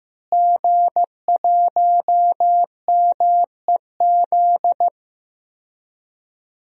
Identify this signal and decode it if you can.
Morse code, heard as a keyed tone, about 15 wpm, 700 Hz. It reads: G1MEZ
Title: DXCC-100 Call Signs - 5 Characters - Repeated 2x - 15wpm